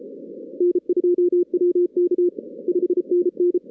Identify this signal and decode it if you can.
Morse code, heard as a keyed tone, about 33 wpm, 380 Hz. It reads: N2WK 5NN
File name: 2016 arrl dx cw / 2016-ARRL-DX-CW-K5TR- - 0-41